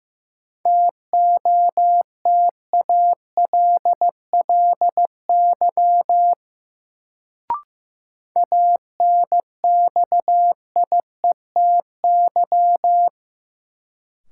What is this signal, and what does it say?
Morse code, heard as a keyed tone, about 15 wpm, 700 Hz. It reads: TOTALLY ANXIETY
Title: Sentences from Top 1600 Words - New Word Drill - 15wpm